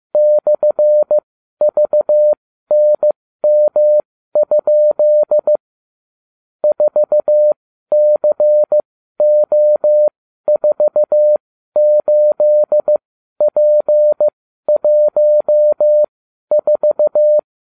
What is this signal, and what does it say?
Morse code, heard as a keyed tone, about 15 wpm, spaced as for 12 wpm, 600 Hz. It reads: /VNM? 4CO48P14